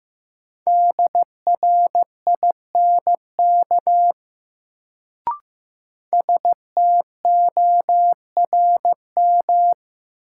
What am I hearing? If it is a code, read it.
Morse code, heard as a keyed tone, about 15 wpm, 700 Hz. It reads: DRINK STORM
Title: Sentences from Top 1500 Words - New Word Drill Alt - 15wpm